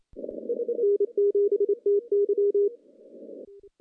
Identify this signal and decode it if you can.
Morse code, heard as a keyed tone, about 26 wpm, 410 Hz. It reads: N7TY
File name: CW-SS-k5tr